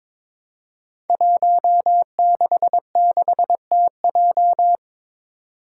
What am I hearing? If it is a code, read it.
Morse code, heard as a keyed tone, about 22 wpm, 700 Hz. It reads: 166TJ
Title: Five Challenging Letter-Number Combinations 22wpm